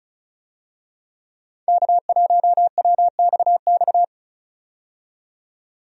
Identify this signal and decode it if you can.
Morse code, heard as a keyed tone, about 36 wpm, 700 Hz. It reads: K1WXX
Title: USA Call Signs - 5 Characters - 35wpm